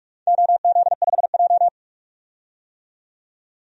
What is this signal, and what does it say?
Morse code, heard as a keyed tone, about 42 wpm, 700 Hz. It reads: OZ5J
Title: DXCC-100 Call Signs - 4 Characters - 45wpm